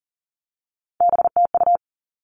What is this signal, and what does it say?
Morse code, heard as a keyed tone, about 40 wpm, 700 Hz. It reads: BTU
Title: All Cut Numbers - 3-Digit 40wpm